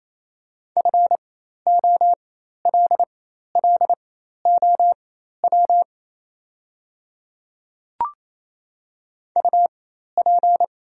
Morse code, heard as a keyed tone, about 28 wpm, spaced as for 13 wpm, 690 Hz. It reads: FOLLOW UP
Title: 4x Character Spacing - Top 200 Words - 28wpm